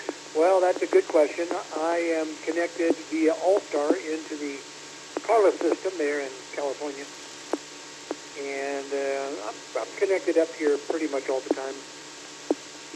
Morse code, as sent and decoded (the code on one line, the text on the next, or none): none